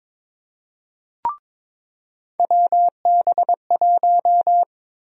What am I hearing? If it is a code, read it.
Morse code, heard as a keyed tone, about 22 wpm, 700 Hz. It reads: WB1
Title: USA Call Sign Prefixes - 22wpm